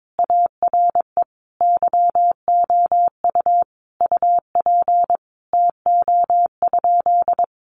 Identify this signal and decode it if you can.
Morse code, heard as a keyed tone, about 22 wpm, 700 Hz. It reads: ARE YOU UP TO?
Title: Sentences from Top 100 Words 22wpm